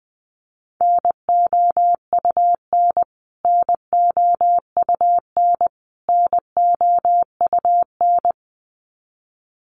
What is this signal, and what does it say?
Morse code, heard as a keyed tone, about 20 wpm, 700 Hz. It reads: NOUN NOUN NOUN
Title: Top 500 Words Repeated 20wpm